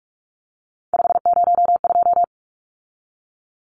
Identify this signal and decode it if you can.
Morse code, heard as a keyed tone, about 44 wpm, 700 Hz. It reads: 502
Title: Numbers - 3-Digit 45wpm